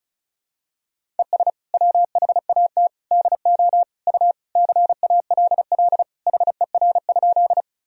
Morse code, heard as a keyed tone, about 36 wpm, 700 Hz. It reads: ES WHAT DO U CALL HER?
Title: Sentences from Top 100 Words and Abbreviations 35wpm